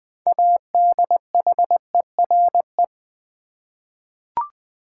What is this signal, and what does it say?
Morse code, heard as a keyed tone, about 20 wpm, 700 Hz. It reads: ADHERE
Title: 6 Letter Words - First 3 Letters Given - 20wpm